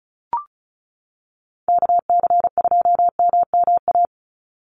Morse code, heard as a keyed tone, about 34 wpm, 700 Hz. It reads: KC2MMA
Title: USA Call Signs 35wpm